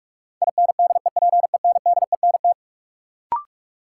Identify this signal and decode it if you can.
Morse code, heard as a keyed tone, about 46 wpm, 700 Hz. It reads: INDEPENDENT
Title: Sentences from Top 900 Words - New Word Drill - 45wpm